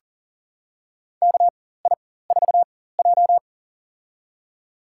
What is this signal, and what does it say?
Morse code, heard as a keyed tone, about 40 wpm, spaced as for 19 wpm, 700 Hz. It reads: KI4J
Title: USA Call Signs - 4 Characters - 4x Character Spacing - 40wpm